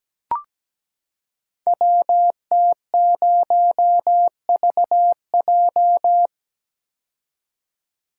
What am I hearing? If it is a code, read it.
Morse code, heard as a keyed tone, about 17 wpm, 700 Hz. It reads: WT0VJ
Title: Code Groups - 5 Letter:Numbers - 17wpm